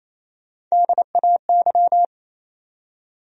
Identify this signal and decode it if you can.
Morse code, heard as a keyed tone, about 28 wpm, 700 Hz. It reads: DAY
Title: Top 100 Words 28wpm